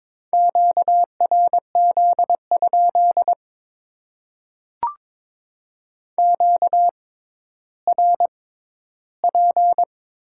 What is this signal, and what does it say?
Morse code, heard as a keyed tone, about 21 wpm, spaced as for 12 wpm, 700 Hz. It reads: QRZ? QRP
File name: Top 90 QSO Elements - 6x Character Spacing - 22wpm